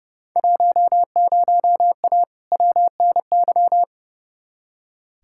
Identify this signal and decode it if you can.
Morse code, heard as a keyed tone, about 30 wpm, 700 Hz. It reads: 10A WNY
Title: Field Day Exchanges - 30wpm